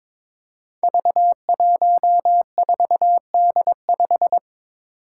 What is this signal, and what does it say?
Morse code, heard as a keyed tone, about 22 wpm, 700 Hz. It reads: V14D5
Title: Five Challenging Letter-Number Combinations 22wpm